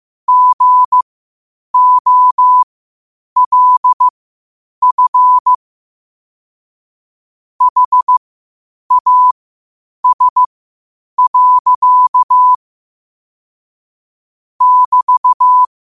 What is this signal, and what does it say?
Morse code, heard as a keyed tone, about 15 wpm, spaced as for 8 wpm, 1000 Hz. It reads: GOLF HAS. <BT>
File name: qotdfiles / 8wpm